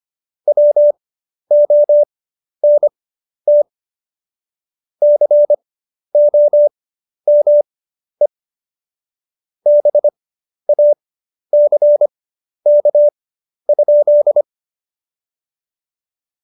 Morse code, heard as a keyed tone, about 25 wpm, spaced as for 11 wpm, 590 Hz. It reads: WONT COME BACK?
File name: Jokes-25-char-10-eff